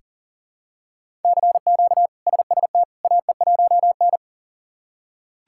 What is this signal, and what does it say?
Morse code, heard as a keyed tone, about 37 wpm, 700 Hz. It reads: CQ SST AE1N
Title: K1USN SST QSO Practice - 40wpm